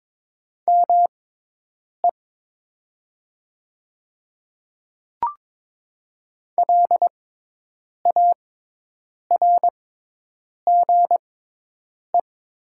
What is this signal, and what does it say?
Morse code, heard as a keyed tone, about 22 wpm, spaced as for 8 wpm, 700 Hz. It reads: ME LARGE
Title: 6x Character Spacing - Top 300 Words - 22wpm